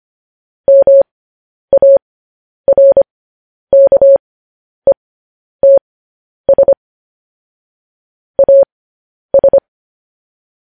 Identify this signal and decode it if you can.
Morse code, heard as a keyed tone, about 25 wpm, spaced as for 10 wpm, 550 Hz. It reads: MARKETS AS